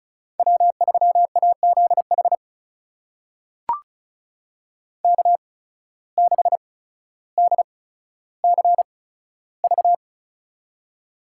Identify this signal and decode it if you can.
Morse code, heard as a keyed tone, about 36 wpm, spaced as for 15 wpm, 700 Hz. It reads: W3AZH K6DCV
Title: USA Call Signs - 5 Characters - 8x Character Spacing - 35wpm